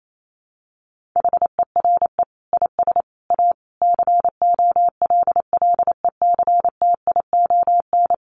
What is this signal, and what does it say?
Morse code, heard as a keyed tone, about 28 wpm, 700 Hz. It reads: HERE IS A COLLECTION
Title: Sentences from Top 1200 Words 28wpm